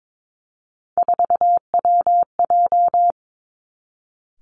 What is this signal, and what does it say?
Morse code, heard as a keyed tone, about 22 wpm, 700 Hz. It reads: 4WJ
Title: Three Challenging Letter-Number Combinations 22wpm